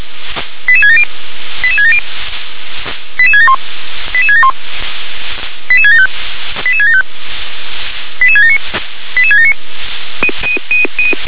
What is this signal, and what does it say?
Five-tone selective call (ZVEI), heard as quick runs of five tones, tones of about 70 ms. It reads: 88670 88670 88671 88671 88675 88675 88679 88679